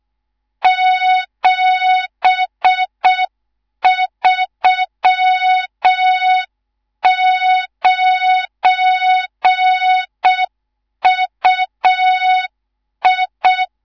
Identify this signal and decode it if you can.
Morse code, heard as a keyed tone, about 6 wpm, 3700 Hz. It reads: 739UI
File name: Morze-HA2MN / hang / 19 GYAKORLAT